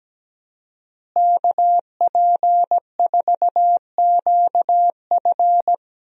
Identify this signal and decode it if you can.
Morse code, heard as a keyed tone, about 17 wpm, 700 Hz. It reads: KP4QF
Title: DXCC-100 Call Signs - 5 Characters - Repeated 2x - 17wpm